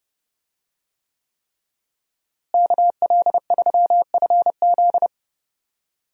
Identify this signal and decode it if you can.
Morse code, heard as a keyed tone, about 31 wpm, 700 Hz. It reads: KL3FZ
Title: USA Call Signs - 5 Characters - Repeated 3x - 30wpm